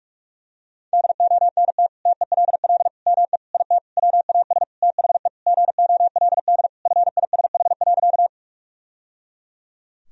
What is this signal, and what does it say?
Morse code, heard as a keyed tone, about 44 wpm, 700 Hz. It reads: DONT TELL ME IT WAS THE GOLD FISH.
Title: Sentences from Top 1000 Words - Review Part 3 - 45wpm